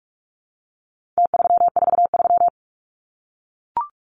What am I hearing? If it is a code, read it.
Morse code, heard as a keyed tone, about 44 wpm, 700 Hz. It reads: T343
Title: Numbers - Contest Style - 4-Digit 45wpm